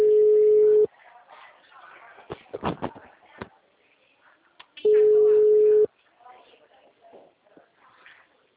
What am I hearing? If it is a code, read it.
Ringback tone (Europe, 425 Hz), with speech.
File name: out- -2003- - - .0